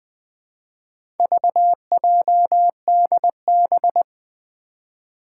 Morse code, heard as a keyed tone, about 20 wpm, 700 Hz. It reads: VJDB
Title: Four Challenging Letter-Number Combinations 20wpm